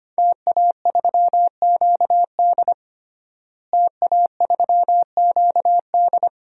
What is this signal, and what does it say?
Morse code, heard as a keyed tone, about 25 wpm, 700 Hz. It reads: TA3QD TA3QD
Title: DXCC-100 Call Signs - 5 Characters - Repeated 3x - 25wpm